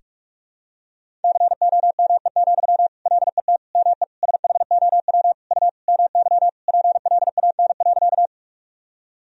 Morse code, heard as a keyed tone, about 46 wpm, 700 Hz. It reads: COME, LET ME SHOW U MY PLAN.
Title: Sentences from Top 500 Words and Abbreviations 45wpm